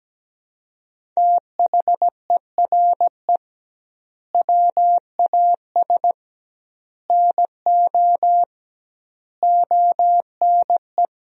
Morse code, heard as a keyed tone, about 17 wpm, 700 Hz. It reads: THERE WAS NO ONE
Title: 2x Word-Spacing - Sentences from Top 200 Words - 17wpm